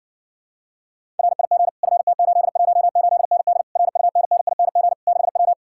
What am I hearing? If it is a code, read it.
Morse code, heard as a keyed tone, about 66 wpm, 700 Hz. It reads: RIG FT818ND RUNNING 6W